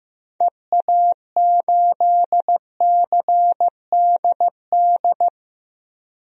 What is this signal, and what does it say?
Morse code, heard as a keyed tone, about 15 wpm, 700 Hz. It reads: EA8CDD
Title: DXCC-100 Call Signs - 6 Characters - Repeated 3x - 15wpm